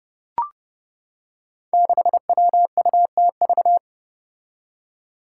Morse code, heard as a keyed tone, about 30 wpm, 700 Hz. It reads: 6WUTV
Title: Five Challenging Letter-Number Combinations 30wpm